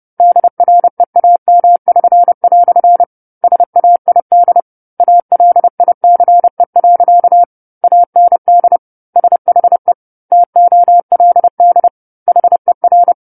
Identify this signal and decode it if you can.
Morse code, heard as a keyed tone, about 30 wpm, 700 Hz. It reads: DREAM<SN>" SAID ALICE. AND SHE TOLD HER
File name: 10-Chapter-10 30 WPM